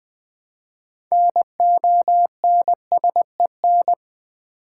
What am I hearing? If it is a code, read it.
Morse code, heard as a keyed tone, about 20 wpm, 700 Hz. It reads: NONSEN
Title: Build A Word - 2 Letters At A Time - 8-Letter Words - 20wpm